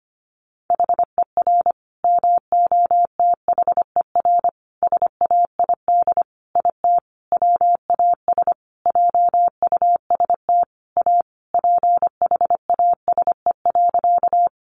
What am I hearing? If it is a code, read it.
Morse code, heard as a keyed tone, about 25 wpm, 700 Hz. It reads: HER MOTHER SAID IT WAS JUST A PHASE.
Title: Sentences from Top 1200 Words 25wpm